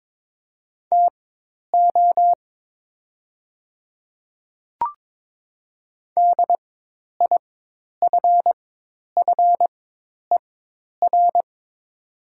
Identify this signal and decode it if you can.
Morse code, heard as a keyed tone, about 22 wpm, spaced as for 10 wpm, 700 Hz. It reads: TO DIFFER
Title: 4x Character Spacing - Top 500 Words - 22wpm